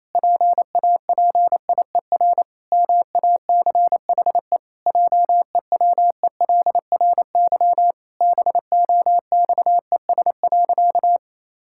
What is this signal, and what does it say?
Morse code, heard as a keyed tone, about 28 wpm, 700 Hz. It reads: PAPIER MACHE JEWELRY BOXES.
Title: Pangram Challenge v2 - 28wpm